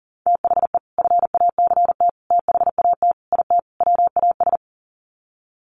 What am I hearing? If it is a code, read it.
Morse code, heard as a keyed tone, about 40 wpm, 700 Hz. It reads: THE FACT THAT IT WAS
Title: Sets of 5 Words Encore2 40wpm